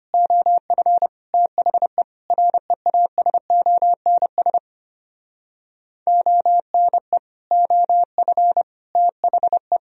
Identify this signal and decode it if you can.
Morse code, heard as a keyed tone, about 32 wpm, 700 Hz. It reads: OF THE REASONS ONE OF THE